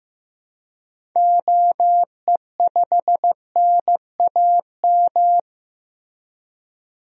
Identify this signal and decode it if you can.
Morse code, heard as a keyed tone, about 15 wpm, 700 Hz. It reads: OE5NAM
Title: DXCC-100 Call Signs - 6 Characters - Repeated 2x - 15wpm